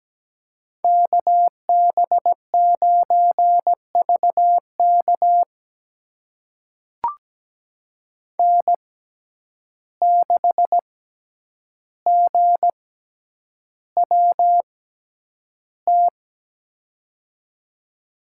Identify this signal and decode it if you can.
Morse code, heard as a keyed tone, about 17 wpm, spaced as for 9 wpm, 700 Hz. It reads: KB9VK N6GWT
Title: USA Call Signs - 5 Characters - 6x Character Spacing - 17wpm